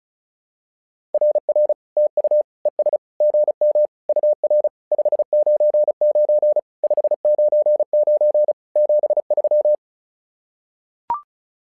Morse code, heard as a keyed tone, about 36 wpm, 600 Hz. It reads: RR TU ES GM UR 599 599 73
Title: SOTA QSO Practice 35wpm